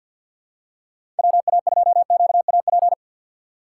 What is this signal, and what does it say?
Morse code, heard as a keyed tone, about 52 wpm, 700 Hz. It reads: WA2QAP
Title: USA Call Signs - 6 Characters - 8x Character Spacing - 50wpm